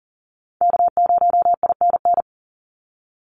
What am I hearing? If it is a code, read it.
Morse code, heard as a keyed tone, about 40 wpm, 700 Hz. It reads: K0INN
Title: Call Signs Repeated 2x - 40wpm